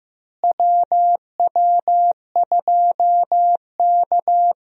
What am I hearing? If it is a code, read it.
Morse code, heard as a keyed tone, about 15 wpm, 700 Hz. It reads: WW2K
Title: USA Call Signs - 4 Characters - Repeated 3x - 15wpm